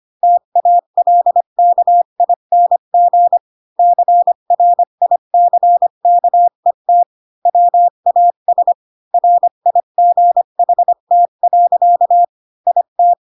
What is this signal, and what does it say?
Morse code, heard as a keyed tone, about 25 wpm, 700 Hz. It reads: TALKING CRICKET WAS RIGHT. IT
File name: Pinocchio / Pinocchio ch5 20 25 ik1hge